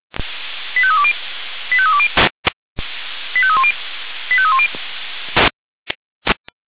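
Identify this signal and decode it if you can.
Five-tone selective call (ZVEI), heard as quick runs of five tones, tones of about 70 ms. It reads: 85320 85320 85310 85310
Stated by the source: ZVEI five-tone call received by SDR.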